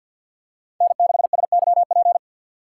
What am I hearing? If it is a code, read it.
Morse code, heard as a keyed tone, about 52 wpm, 700 Hz. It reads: N6SYP
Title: USA Call Signs - 5 Characters - 2x Character Spacing - 50wpm